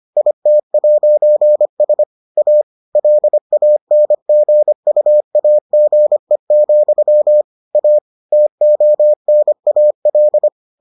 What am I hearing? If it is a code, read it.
Morse code, heard as a keyed tone, about 25 wpm, 600 Hz. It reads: IT'S A LANGUAGE, A TONAL